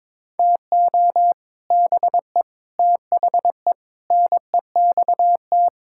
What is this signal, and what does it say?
Morse code, heard as a keyed tone, about 22 wpm, 700 Hz. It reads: TO BE THE NEXT
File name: Sets of 9 Words - 22wpm